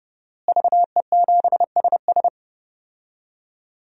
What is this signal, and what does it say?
Morse code, heard as a keyed tone, about 32 wpm, 700 Hz. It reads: VE7SS
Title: DXCC-100 Call Signs - 5 Characters - Repeated 3x - 30wpm